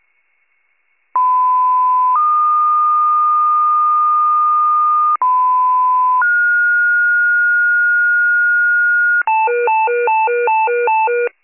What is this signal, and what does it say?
Two-tone page, heard as a one-second tone, then a three-second tone, about 1010 then 1220 Hz, about 1010 then 1430 Hz, with noise. It is heard from 1.1 to 9.2 s.
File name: pager / 94 FIRE / wcalert